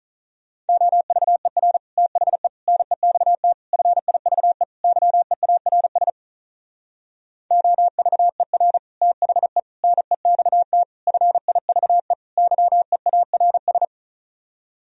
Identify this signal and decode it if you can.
Morse code, heard as a keyed tone, about 44 wpm, 700 Hz. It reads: OVER THE NEXT FIVE YEARS OVER THE NEXT FIVE YEARS
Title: Speed-Racing - Sets of 5 Words 35wpm